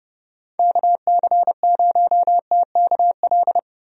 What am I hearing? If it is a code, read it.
Morse code, heard as a keyed tone, about 30 wpm, 700 Hz. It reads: KC0TKL
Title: USA Call Signs - 6 Characters - 30wpm